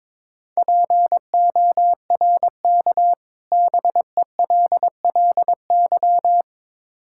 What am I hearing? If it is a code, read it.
Morse code, heard as a keyed tone, about 22 wpm, 700 Hz. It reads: PORK BELLY
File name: Difficult Words with Hints - 22wpm